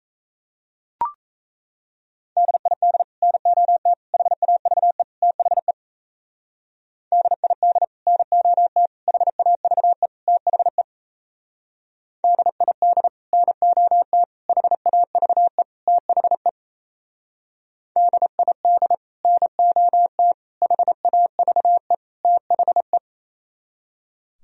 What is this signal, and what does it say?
Morse code, heard as a keyed tone, about 35 wpm, 700 Hz. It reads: DID NOT HAVE THE DID NOT HAVE THE DID NOT HAVE THE DID NOT HAVE THE
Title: Speed-Racing - Sets of 4 Words 28wpm